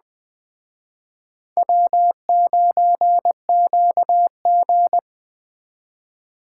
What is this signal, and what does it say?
Morse code, heard as a keyed tone, about 21 wpm, 700 Hz. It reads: W9QG
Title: Speed Racing - USA Call Signs 17wpm